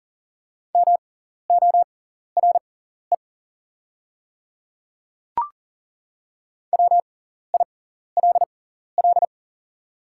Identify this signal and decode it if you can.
Morse code, heard as a keyed tone, about 40 wpm, spaced as for 14 wpm, 690 Hz. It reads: MORE WILL
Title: 6x Character Spacing - Top 100 Words - 40wpm